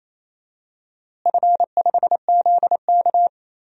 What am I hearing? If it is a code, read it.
Morse code, heard as a keyed tone, about 28 wpm, 700 Hz. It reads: F5ZK
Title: DXCC-100 Call Signs - 4 Characters - Repeated 2x - 28wpm